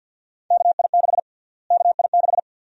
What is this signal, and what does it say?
Morse code, heard as a keyed tone, about 51 wpm, 700 Hz. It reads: KI6 KI6
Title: USA Call Sign Prefixes - Repeated 2x - 50wpm